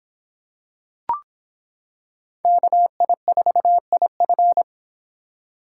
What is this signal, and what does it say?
Morse code, heard as a keyed tone, about 26 wpm, 700 Hz. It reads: KI4IF